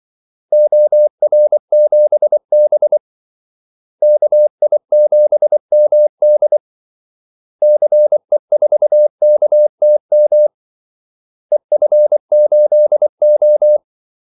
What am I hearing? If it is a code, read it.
Morse code, heard as a keyed tone, about 24 wpm, 600 Hz. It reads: OR7B KI7MD CE4KTM EF8O
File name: Callsigns-24wpm